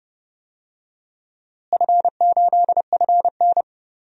Morse code, heard as a keyed tone, about 29 wpm, 700 Hz. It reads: F8FN
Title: DXCC-100 Call Signs - 4 Characters - Repeated 2x - 30wpm